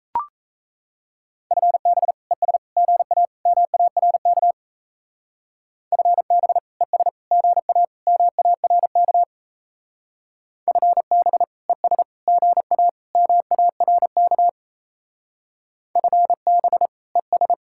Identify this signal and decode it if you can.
Morse code, heard as a keyed tone, about 33 wpm, 700 Hz. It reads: FB ES GA MARK FB ES GA MARK FB ES GA MARK FB ES
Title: Speed-Racing Common QSO Phrases - 28wpm